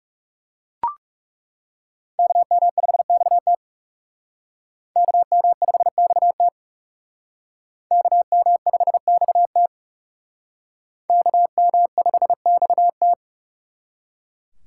Morse code, heard as a keyed tone, about 40 wpm, 700 Hz. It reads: KM5XT KM5XT KM5XT KM5XT